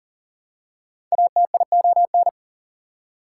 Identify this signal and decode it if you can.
Morse code, heard as a keyed tone, about 40 wpm, 700 Hz. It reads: ATION
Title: Prefixes and Suffixes 40wpm